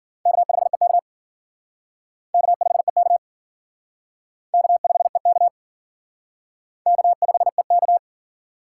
Morse code, heard as a keyed tone, about 52 wpm, 700 Hz. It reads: K5EK K5EK K5EK K5EK